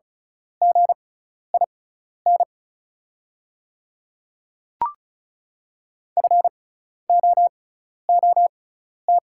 Morse code, heard as a keyed tone, about 36 wpm, spaced as for 12 wpm, 700 Hz. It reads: GIN FOOT
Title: 6x Character Spacing - Top 500 Words - 35wpm